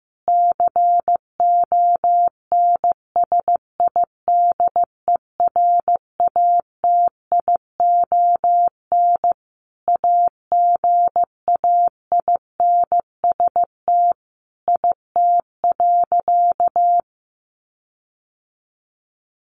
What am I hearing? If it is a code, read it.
Morse code, heard as a keyed tone, about 15 wpm, 700 Hz. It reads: CONSIDERATION AGAINST IT.
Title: Sentences from Top 1500 Words 15wpm